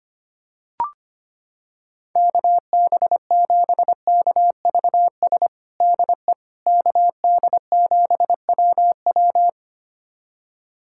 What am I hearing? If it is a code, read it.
Morse code, heard as a keyed tone, about 25 wpm, 700 Hz. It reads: KB7KVS DE KD7WW
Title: USA Callsign DE Callsign 25wpm